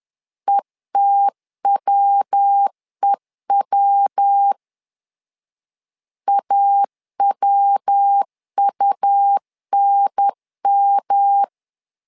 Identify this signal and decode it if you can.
Morse code, heard as a keyed tone, about 10 wpm, 800 Hz. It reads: ETWEW AWUNM
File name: LEZIONI / 3 MWRO / 3 MW 10wpm 1959